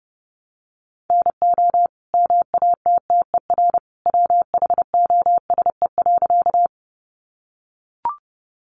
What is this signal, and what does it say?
Morse code, heard as a keyed tone, about 30 wpm, 700 Hz. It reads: NO MATTER WHOSE.
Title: Sentences from Top 1200 Words 30wpm